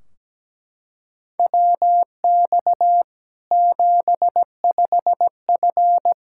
Morse code, heard as a keyed tone, about 17 wpm, 700 Hz. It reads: WX 75F
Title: Practice QSO - Exchange 2 - Info - 17wpm